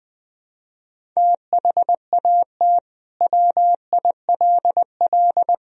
Morse code, heard as a keyed tone, about 21 wpm, 700 Hz. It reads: THAT WILL
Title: Sentences from Top 1000 Words - Review Part 1 - 20wpm